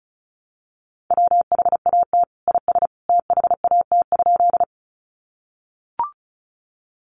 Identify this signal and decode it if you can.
Morse code, heard as a keyed tone, about 34 wpm, 700 Hz. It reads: WHAT IS THAT?
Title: Sentences from Top 100 Words 35wpm